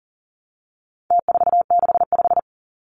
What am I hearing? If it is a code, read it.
Morse code, heard as a keyed tone, about 39 wpm, 700 Hz. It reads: T465
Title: Numbers - Contest Style - 4-Digit 40wpm